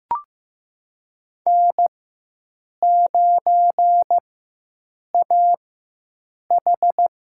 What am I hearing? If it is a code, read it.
Morse code, heard as a keyed tone, about 15 wpm, spaced as for 7 wpm, 700 Hz. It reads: N9AH
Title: USA Call Signs - 4 Characters - 4x Character Spacing - 15wpm